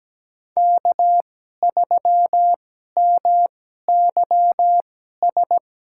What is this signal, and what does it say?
Morse code, heard as a keyed tone, about 17 wpm, spaced as for 12 wpm, 700 Hz. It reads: K3MYS
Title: USA Call Signs - 5 Characters - 2x Character Spacing - 17wpm